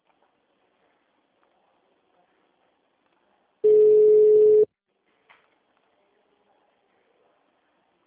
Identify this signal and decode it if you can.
Ringback tone (Europe, 425 Hz).
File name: out- -2005- - - .59986